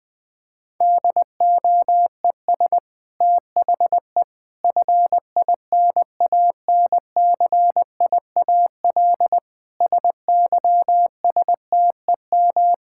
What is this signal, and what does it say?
Morse code, heard as a keyed tone, about 20 wpm, 700 Hz. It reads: DOES THE FINANCIAL SYSTEM
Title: Sentences from Top 1000 Words - Review Part 5 - 20wpm